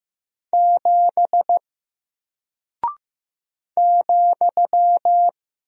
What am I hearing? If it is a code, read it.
Morse code, heard as a keyed tone, about 15 wpm, 700 Hz. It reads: 7 ,
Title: Single Character (Letter:Number:Punctuation:Prosign) Alt 15wpm